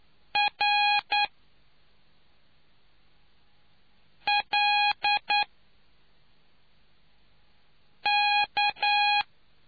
Morse code, heard as a keyed tone, about 9 wpm, 820 Hz. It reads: R L K